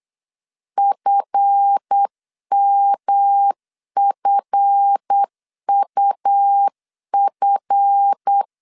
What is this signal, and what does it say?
Morse code, heard as a keyed tone, about 8 wpm, 800 Hz. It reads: FMFUF
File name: LEZIONI / 7 GF9Z / 7 GF 8wpm 3108